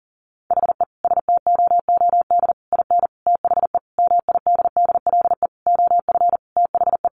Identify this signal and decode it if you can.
Morse code, heard as a keyed tone, about 40 wpm, 700 Hz. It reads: HE STOOD IN THE MIDDLE OF THE
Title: Sentences from Top 1400 Words 40wpm